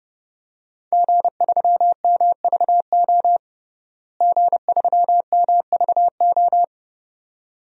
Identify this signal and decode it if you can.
Morse code, heard as a keyed tone, about 32 wpm, 700 Hz. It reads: G3MVO G3MVO
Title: DXCC-100 Call Signs - 5 Characters - Repeated 2x - 30wpm